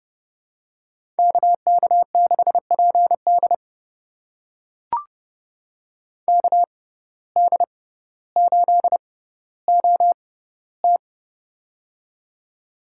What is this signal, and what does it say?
Morse code, heard as a keyed tone, about 30 wpm, spaced as for 15 wpm, 700 Hz. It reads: KK6PD KD8OT
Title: USA Call Signs - 5 Characters - 6x Character Spacing - 30wpm